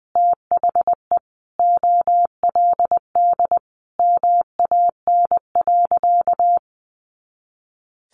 Morse code, heard as a keyed tone, about 20 wpm, 700 Hz. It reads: THE OLD MAN.
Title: Sentences from Top 500 Words 20wpm